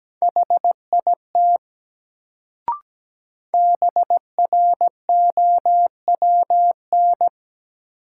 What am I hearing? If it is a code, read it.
Morse code, heard as a keyed tone, about 17 wpm, 700 Hz. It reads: HIT BROWN
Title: Sentences from Top 1200 Words - New Word Drill - 17wpm